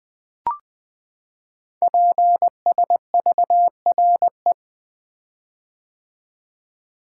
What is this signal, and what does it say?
Morse code, heard as a keyed tone, about 21 wpm, 700 Hz. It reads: PSVRE
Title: Code Groups - 5 Letter:Numbers - 20wpm